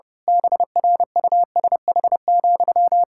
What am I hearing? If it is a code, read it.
Morse code, heard as a keyed tone, about 30 wpm, 700 Hz. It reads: BRUSH,
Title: Pangram Challenge v2 - 30wpm